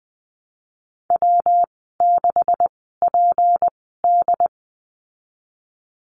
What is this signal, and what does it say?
Morse code, heard as a keyed tone, about 19 wpm, spaced as for 14 wpm, 700 Hz. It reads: W6PD
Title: 167 Lesson 27g - Number 7 - Call Signs 20wpm